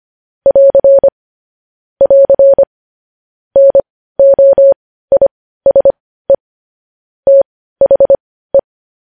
Morse code, heard as a keyed tone, about 25 wpm, spaced as for 15 wpm, 550 Hz. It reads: <AR> <AR> NOISE THE
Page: news-headlines-in-morse-code-at-15-wpm